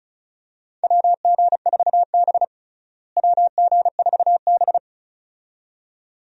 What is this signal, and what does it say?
Morse code, heard as a keyed tone, about 36 wpm, 700 Hz. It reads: WG4B WG4B
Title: USA Call Signs - 4 Characters - Repeated 2x - 35wpm